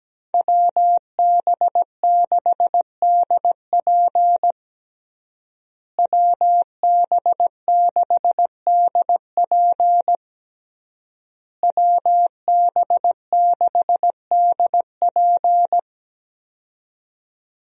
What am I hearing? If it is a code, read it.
Morse code, heard as a keyed tone, about 17 wpm, 700 Hz. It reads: WB6DP WB6DP WB6DP
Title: USA Call Signs - 5 Characters - Repeated 3x - 17wpm